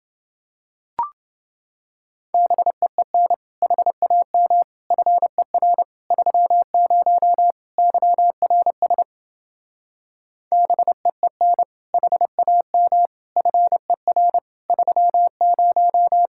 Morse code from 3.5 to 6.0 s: HAM FER

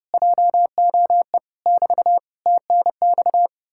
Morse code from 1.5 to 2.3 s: <BT>

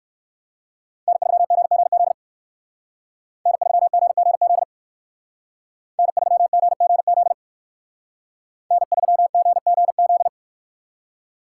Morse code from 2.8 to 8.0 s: N3GGZ N3GGZ

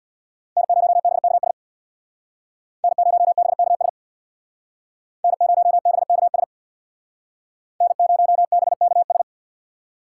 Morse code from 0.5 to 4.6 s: N0BKS N0BKS

